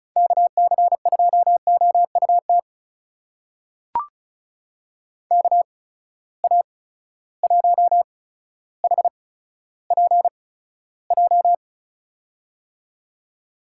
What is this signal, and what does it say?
Morse code, heard as a keyed tone, about 36 wpm, spaced as for 15 wpm, 700 Hz. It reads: KC2OUT KA1HPJ
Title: USA Call Signs - 6 Characters - 8x Character Spacing - 35wpm